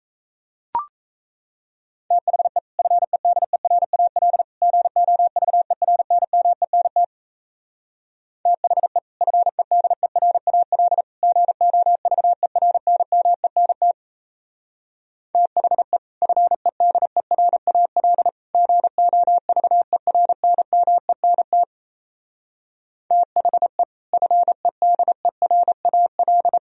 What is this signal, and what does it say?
Morse code, heard as a keyed tone, about 35 wpm, 700 Hz. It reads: THE FEDERAL GOVERNMENT THE FEDERAL GOVERNMENT THE FEDERAL GOVERNMENT THE FEDERAL
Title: Speed-Racing - Sets of 3 Words 28wpm